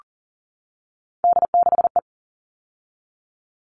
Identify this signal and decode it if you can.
Morse code, heard as a keyed tone, about 40 wpm, 700 Hz. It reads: D6E
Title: All Cut Numbers - 3-Digit 40wpm